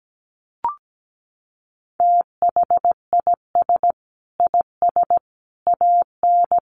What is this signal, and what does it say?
Morse code, heard as a keyed tone, about 17 wpm, 700 Hz. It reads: THIS IS AN
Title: Sentences from Top 900 Words 17wpm